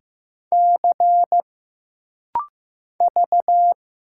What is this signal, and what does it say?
Morse code, heard as a keyed tone, about 15 wpm, 700 Hz. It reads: C V
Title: Single Letter-Number Alt - Rapid-Fire 15wpm